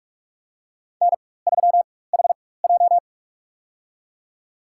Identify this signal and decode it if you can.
Morse code, heard as a keyed tone, about 44 wpm, spaced as for 21 wpm, 700 Hz. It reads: N3HJ
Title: USA Call Signs - 4x Character Spacing - 45wpm